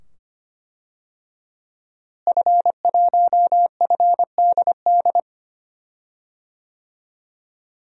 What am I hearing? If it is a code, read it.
Morse code, heard as a keyed tone, about 25 wpm, 700 Hz. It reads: F1FDD
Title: DXCC-100 Call Signs - 5 Characters - Repeated 2x - 25wpm